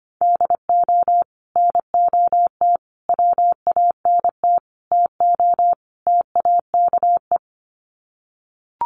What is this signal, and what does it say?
Morse code, heard as a keyed tone, about 25 wpm, 700 Hz. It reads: DO NOT WANT TO TAKE
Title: Sets of 5 Words Encore2 25wpm